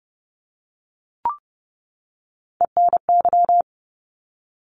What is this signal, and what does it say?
Morse code, heard as a keyed tone, about 30 wpm, 700 Hz. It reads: ENY
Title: ARRL Sections 30wpm